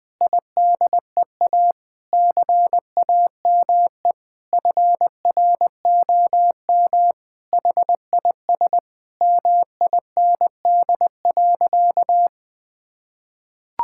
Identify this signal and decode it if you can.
Morse code, heard as a keyed tone, about 20 wpm, 700 Hz. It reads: IDEA CAME FROM HIS MIND.
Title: Sentences with 2-Word Phrases - 20wpm